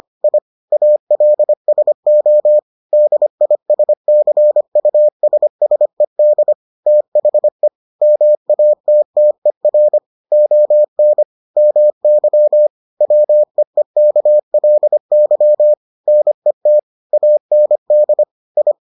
Morse code, heard as a keyed tone, about 25 wpm, 600 Hz. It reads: I ALSO DISCUSSED THE MATTER ON MY WEEKLY NET AND I